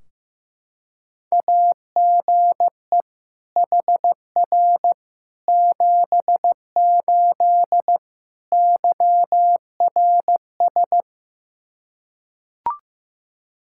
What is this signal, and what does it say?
Morse code, heard as a keyed tone, about 15 wpm, 700 Hz. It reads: AGE HR 78 YRS
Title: K1USN SST QSO Practice 15wpm